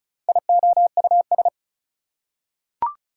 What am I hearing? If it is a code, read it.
Morse code, heard as a keyed tone, about 34 wpm, 700 Hz. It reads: IOUS
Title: Prefixes and Suffixes 35wpm